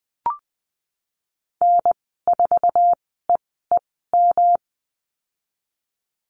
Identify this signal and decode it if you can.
Morse code, heard as a keyed tone, about 20 wpm, spaced as for 15 wpm, 700 Hz. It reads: N4EEM
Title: 179 Lesson 29g - Slash - Call Signs 20wpm